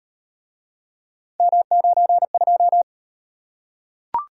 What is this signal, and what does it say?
Morse code, heard as a keyed tone, about 38 wpm, 700 Hz. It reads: M92